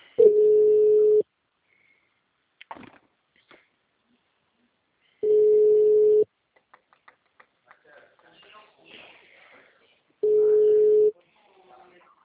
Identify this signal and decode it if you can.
Ringback tone (Europe, 425 Hz), with speech.
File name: out- -2003- - - .57605